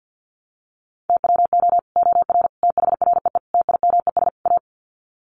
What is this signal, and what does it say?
Morse code, heard as a keyed tone, about 50 wpm, 700 Hz. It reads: TWO OR THREE TIMES A
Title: Sets of 5 Words Encore2 50wpm